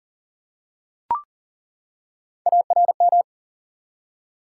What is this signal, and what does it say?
Morse code, heard as a keyed tone, about 39 wpm, 700 Hz. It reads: ARM
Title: Three Letter Words - 40wpm